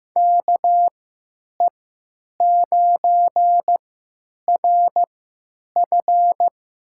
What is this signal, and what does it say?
Morse code, heard as a keyed tone, about 15 wpm, spaced as for 9 wpm, 700 Hz. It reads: KE9RF
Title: USA Call Signs - 3x Character Spacing - 15wpm